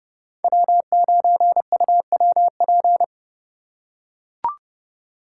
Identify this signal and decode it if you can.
Morse code, heard as a keyed tone, about 29 wpm, 700 Hz. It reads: W9UWP